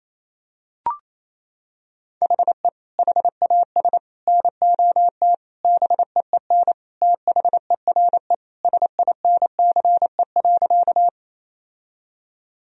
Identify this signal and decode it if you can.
Morse code, heard as a keyed tone, about 28 wpm, 700 Hz. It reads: HE HAS NOT BEEN THERE SINCE.
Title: Sentences from Top 1000 Words - Review Part 1 - 28wpm